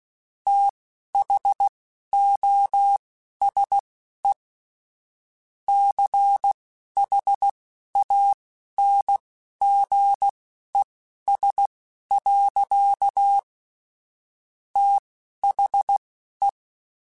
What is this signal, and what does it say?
Morse code, heard as a keyed tone, about 16 wpm, spaced as for 11 wpm, 800 Hz. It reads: THOSE CHANGES. THE
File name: Technical Subjects / Technical Subjects-mp3s / XC @10